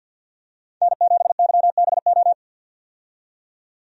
Morse code, heard as a keyed tone, about 53 wpm, 700 Hz. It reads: N7YBO
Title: USA Call Signs - 5 Characters - 50wpm